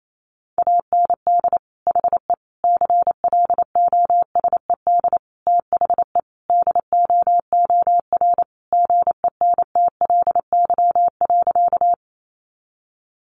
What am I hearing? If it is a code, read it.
Morse code, heard as a keyed tone, about 28 wpm, 700 Hz. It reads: AND HE CLOSED THE DOOR GENTLY.
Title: Sentences from Top 1500 Words 28wpm